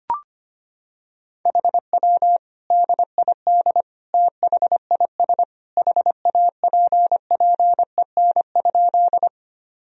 Morse code, heard as a keyed tone, about 25 wpm, 700 Hz. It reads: HW DID THIS HAPPEN?
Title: Sentences from Top 400 Words and Abbreviations 25wpm